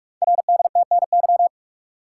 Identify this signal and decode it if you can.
Morse code, heard as a keyed tone, about 44 wpm, 700 Hz. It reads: RDTNY
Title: Code Groups - 5 Letters - 45wpm